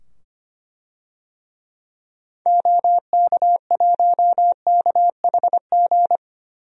Morse code, heard as a keyed tone, about 25 wpm, 700 Hz. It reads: OK1KHG
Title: DXCC-100 Call Signs - 6 Characters - 25wpm